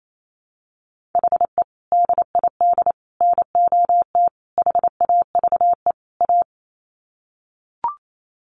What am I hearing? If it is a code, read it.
Morse code, heard as a keyed tone, about 28 wpm, 700 Hz. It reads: HE DID NOT HAVE A
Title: Sets of 5 Words Encore 28wpm